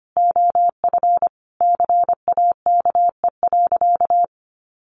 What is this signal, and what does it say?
Morse code, heard as a keyed tone, about 25 wpm, 700 Hz. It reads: OF CAKE.
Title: Additional Easy Phrases 25wpm